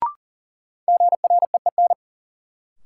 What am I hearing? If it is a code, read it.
Morse code, heard as a keyed tone, about 40 wpm, 700 Hz. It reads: GREEN
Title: Sentences from Top 500 Words - New Word Drill - 40wpm